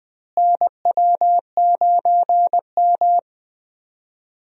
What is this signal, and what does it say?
Morse code, heard as a keyed tone, about 20 wpm, 700 Hz. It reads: NW9M
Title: USA Call Signs - 4 Characters - Repeated 2x - 20wpm